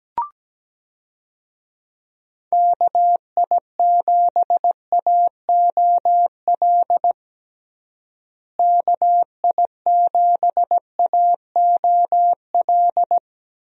Morse code, heard as a keyed tone, about 17 wpm, 700 Hz. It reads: KI7AOL KI7AOL
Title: USA Call Signs - 6 Characters - Repeated 3x - 17wpm